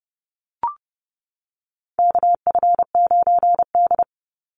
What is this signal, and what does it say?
Morse code, heard as a keyed tone, about 30 wpm, 700 Hz. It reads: KF9D
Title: Call Signs Repeated 2x - 30wpm